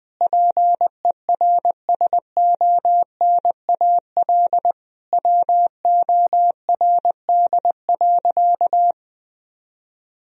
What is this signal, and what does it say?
Morse code, heard as a keyed tone, about 21 wpm, 700 Hz. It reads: PERSONAL WORD.
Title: Sentences from Top 1000 Words - Review Part 4 - 20wpm